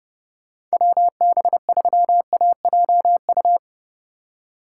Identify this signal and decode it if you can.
Morse code, heard as a keyed tone, about 37 wpm, 700 Hz. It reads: WB3AJU
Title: Speed Racing - USA 2x3 Call Signs 30wpm